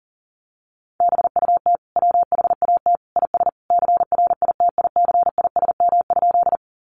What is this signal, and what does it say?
Morse code, heard as a keyed tone, about 40 wpm, 700 Hz. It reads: BUT WHAT IS CRITICISM?
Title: Sentences from Top 1300 Words 40wpm